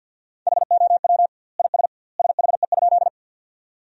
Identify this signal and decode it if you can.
Morse code, heard as a keyed tone, about 52 wpm, 700 Hz. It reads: HOW IS SHE?
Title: 2x Word-Spacing - Sentences from Top 100 Words - 50wpm